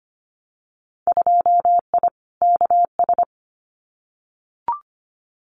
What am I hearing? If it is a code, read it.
Morse code, heard as a keyed tone, about 25 wpm, 700 Hz. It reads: 2I KS
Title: Winter Field Day Exchanges Encore 25wpm